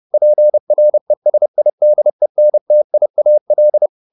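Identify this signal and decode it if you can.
Morse code, heard as a keyed tone, about 30 wpm, 600 Hz. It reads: PRESIDENTIAL
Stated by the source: Morse code at 30 wpm, the novel Never by Ken Follett in English, from CW med Gnister.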